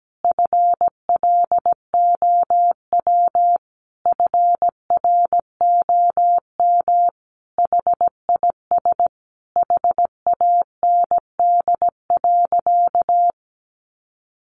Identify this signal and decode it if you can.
Morse code, heard as a keyed tone, about 17 wpm, 700 Hz. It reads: FLOW FROM HIS HAND.
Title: Sentences from Top 1200 Words 17wpm